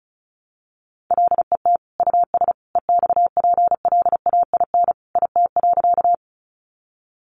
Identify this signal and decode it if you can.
Morse code, heard as a keyed tone, about 34 wpm, 700 Hz. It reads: LET US EXPLAIN IT.
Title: Sentences from Top 1100 Words 35wpm